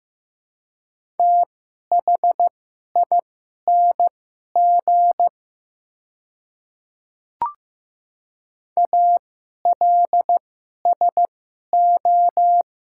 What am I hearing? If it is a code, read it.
Morse code, heard as a keyed tone, about 15 wpm, spaced as for 11 wpm, 700 Hz. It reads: THING ALSO
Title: 2x Character Spacing - Top 200 Words - 15wpm